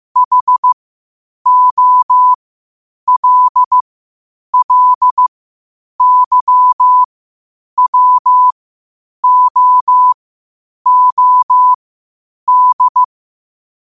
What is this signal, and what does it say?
Morse code, heard as a keyed tone, about 15 wpm, spaced as for 8 wpm, 1000 Hz. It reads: HOLLYWOOD
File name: qotdfiles / 8wpm